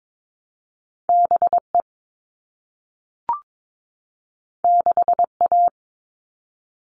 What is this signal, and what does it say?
Morse code, heard as a keyed tone, about 22 wpm, 700 Hz. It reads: BE 6A
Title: All Cut Numbers - 2-Digit 22wpm